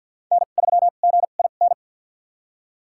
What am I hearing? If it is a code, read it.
Morse code, heard as a keyed tone, about 52 wpm, spaced as for 36 wpm, 690 Hz. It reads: N3GIN
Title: USA Call Signs - 5 Characters - 2x Character Spacing - 50wpm